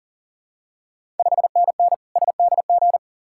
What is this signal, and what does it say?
Morse code, heard as a keyed tone, about 40 wpm, 690 Hz. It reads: 5NN SDG
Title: Ontario QSO Party Exchange 40wpm